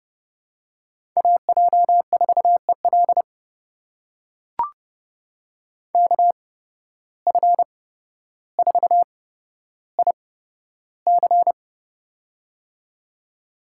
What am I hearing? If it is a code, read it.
Morse code, heard as a keyed tone, about 31 wpm, spaced as for 13 wpm, 700 Hz. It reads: AJ4EL KF4IC
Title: USA Call Signs - 5 Characters - 8x Character Spacing - 30wpm